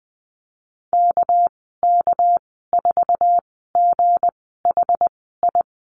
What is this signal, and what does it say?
Morse code, heard as a keyed tone, about 20 wpm, spaced as for 14 wpm, 700 Hz. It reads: KK4GHI
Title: 207 Lesson 33g - Letter j - Call Signs 20wpm